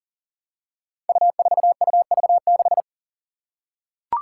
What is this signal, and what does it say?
Morse code, heard as a keyed tone, about 40 wpm, 700 Hz. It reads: U4UV6
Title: Five Challenging Letter-Number Combinations 40wpm